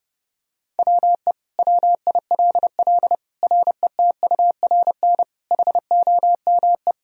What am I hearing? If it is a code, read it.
Morse code, heard as a keyed tone, about 30 wpm, 700 Hz. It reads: WE WILL RETURN HOME
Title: Sentences from Top 1000 Words - Review Part 3 - 30wpm